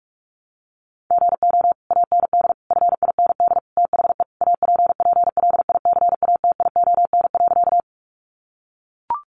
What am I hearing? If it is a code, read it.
Morse code, heard as a keyed tone, about 44 wpm, 700 Hz. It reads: GO AND FIND THE APPLICATION.
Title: Sentences from Top 1200 Words 45wpm